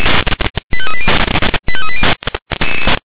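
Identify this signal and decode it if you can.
Five-tone selective call (ZVEI), heard as quick runs of five tones, tones of about 70 ms. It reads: 85380 85380
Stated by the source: Five-tone ZVEI selective call received on an SDR.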